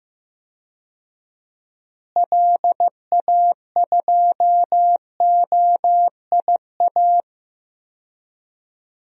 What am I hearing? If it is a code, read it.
Morse code, heard as a keyed tone, about 15 wpm, 700 Hz. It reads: LA2OIA
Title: DXCC-100 Call Signs - 6 Characters - Repeated 2x - 15wpm